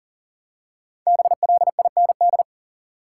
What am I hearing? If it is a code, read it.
Morse code, heard as a keyed tone, about 40 wpm, 700 Hz. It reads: BLIND
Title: Five Letter Words - 40wpm